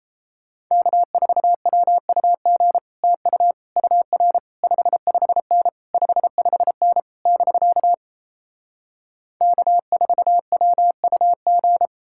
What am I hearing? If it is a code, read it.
Morse code, heard as a keyed tone, about 37 wpm, 700 Hz. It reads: K4WUG TU UR 55N 55N <BK> K4WUG